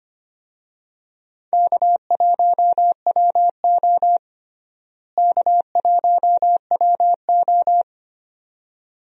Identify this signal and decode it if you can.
Morse code, heard as a keyed tone, about 25 wpm, 700 Hz. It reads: K1WO K1WO
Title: USA Call Signs - 4 Characters - Repeated 2x - 25wpm